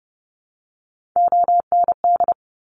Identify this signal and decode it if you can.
Morse code, heard as a keyed tone, about 31 wpm, 700 Hz. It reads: OND
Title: Top Three Letter Combinations in Words 30wpm